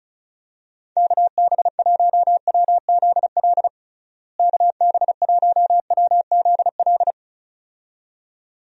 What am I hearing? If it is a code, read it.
Morse code, heard as a keyed tone, about 36 wpm, 700 Hz. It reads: KB1WZL KB1WZL
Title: USA Call Signs - 6 Characters - Repeated 2x - 35wpm